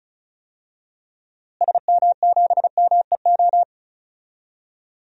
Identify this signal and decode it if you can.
Morse code, heard as a keyed tone, about 36 wpm, 700 Hz. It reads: SM7MEO
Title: DXCC-100 Call Signs - 6 Characters - Repeated 3x - 35wpm